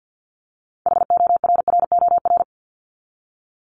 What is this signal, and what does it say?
Morse code, heard as a keyed tone, about 50 wpm, 700 Hz. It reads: HORROR
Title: Top 1501-1750 Words 50wpm